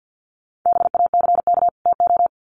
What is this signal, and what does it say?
Morse code, heard as a keyed tone, about 50 wpm, 700 Hz. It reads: BACK TO
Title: Sets of 2 Words 50wpm